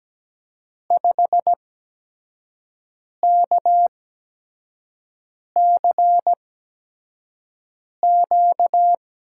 Morse code, heard as a keyed tone, about 17 wpm, 700 Hz. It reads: 5 K C Q
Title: USA Call Signs - 5 Characters - 8x Character Spacing - 17wpm